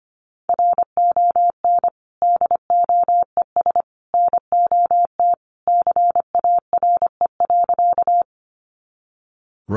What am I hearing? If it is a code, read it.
Morse code, heard as a keyed tone, about 25 wpm, 700 Hz. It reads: RON DOES NOT CARE.
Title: Additional Easy Phrases 25wpm